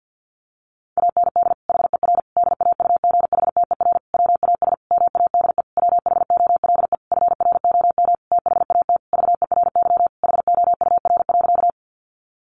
Morse code, heard as a keyed tone, about 48 wpm, 700 Hz. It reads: AND HER DAUGHTER WAS MADE WHOLE FROM THAT VERY HOUR.
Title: Sentences from Top 1500 Words 50wpm